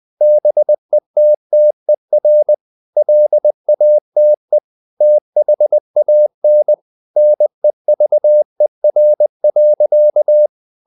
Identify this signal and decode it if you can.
Morse code, heard as a keyed tone, about 20 wpm, 600 Hz. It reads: BETTER LATE THAN NEVER.
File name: CW / Practice / EnglishSayings20